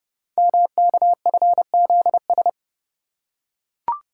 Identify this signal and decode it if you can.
Morse code, heard as a keyed tone, about 30 wpm, 700 Hz. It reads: MKFZS
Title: Code Groups - 5 Letters - 30wpm